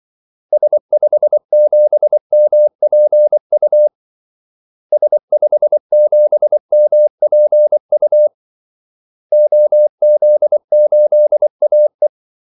Morse code, heard as a keyed tone, about 24 wpm, 600 Hz. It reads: S57MPU S57MPU OZ8AE
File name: Callsigns-24wpm